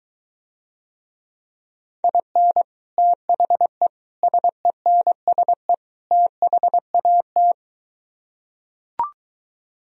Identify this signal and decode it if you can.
Morse code, heard as a keyed tone, about 23 wpm, 700 Hz. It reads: IN THE SENSE THAT